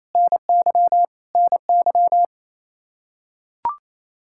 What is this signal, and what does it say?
Morse code, heard as a keyed tone, about 27 wpm, 700 Hz. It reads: NY NY
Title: K1USN SST QSO Practice 28wpm